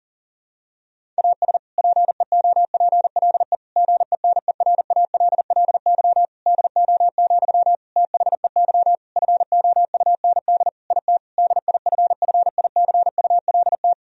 Morse code, heard as a keyed tone, about 40 wpm, 700 Hz. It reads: AS PEOPLE GENERALLY DO, THEY FOUND IT DIFFICULT
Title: Sentences from Top 1000 Words - Review Part 4 - 40wpm